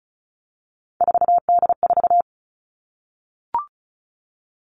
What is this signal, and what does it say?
Morse code, heard as a keyed tone, about 34 wpm, 700 Hz. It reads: 4D4
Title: All Cut Numbers - 3-Digit 35wpm